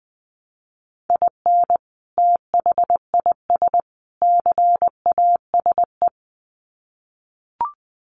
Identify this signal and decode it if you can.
Morse code, heard as a keyed tone, about 20 wpm, 700 Hz. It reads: IN THIS CASE
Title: Sets of 3 Words 20wpm